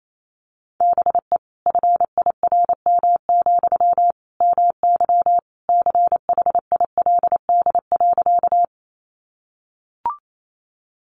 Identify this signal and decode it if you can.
Morse code, heard as a keyed tone, about 28 wpm, 700 Hz. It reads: BE FIRM, MY CHILD.
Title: Sentences from Top 1400 Words 28wpm